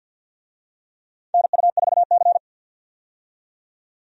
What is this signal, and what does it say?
Morse code, heard as a keyed tone, about 50 wpm, 700 Hz. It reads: NU4C
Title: USA Call Signs - 4 Characters - 50wpm